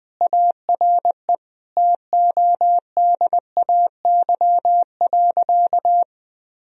Morse code, heard as a keyed tone, about 20 wpm, 700 Hz. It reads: ARE TODAY.
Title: Sentences from Top 1000 Words - Review Part 4 - 20wpm